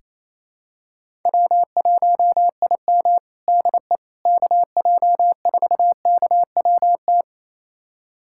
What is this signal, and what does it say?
Morse code, heard as a keyed tone, about 28 wpm, 700 Hz. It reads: W1IM DE KJ4KWT
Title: USA Callsign DE Callsign 28wpm